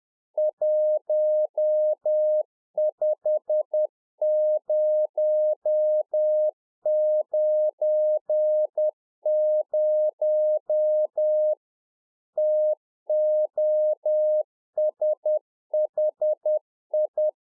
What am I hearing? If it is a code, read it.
Morse code, heard as a keyed tone, about 10 wpm, 610 Hz. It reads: 15090 TOSHI
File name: 2011-04-mp3 / NEW MEMBERS 10WPM